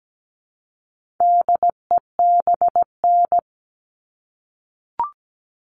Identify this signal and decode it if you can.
Morse code, heard as a keyed tone, about 17 wpm, 700 Hz. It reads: DEBN
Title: All Cut Numbers - 4-Digit 17wpm